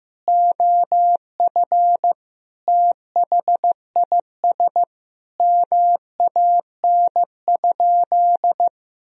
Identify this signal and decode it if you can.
Morse code, heard as a keyed tone, about 15 wpm, 700 Hz. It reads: OF THIS MAN?
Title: Sentences from Top 1000 Words - Review Part 3 - 15wpm